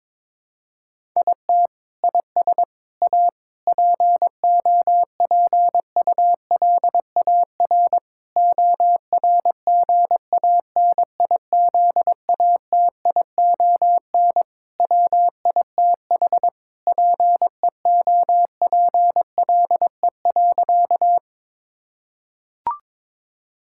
Morse code, heard as a keyed tone, about 22 wpm, 700 Hz. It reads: IT IS A POPULAR ORGANIZATION WITH PEOPLE.
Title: Sentences from Top 1000 Words - Review Part 5 - 22wpm